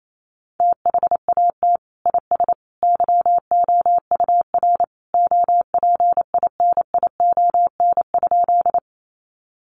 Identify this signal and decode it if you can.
Morse code, heard as a keyed tone, about 28 wpm, 700 Hz. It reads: THAT IS YOUR OPINION?
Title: Sentences from Top 1200 Words 28wpm